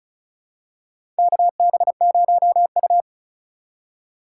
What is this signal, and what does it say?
Morse code, heard as a keyed tone, about 36 wpm, 700 Hz. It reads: KB0U
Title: USA Call Signs - 2x Character Spacing - 35wpm